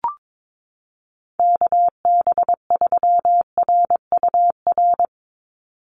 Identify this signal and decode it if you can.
Morse code, heard as a keyed tone, about 22 wpm, 700 Hz. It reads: KB3RUR
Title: USA Call Signs 22wpm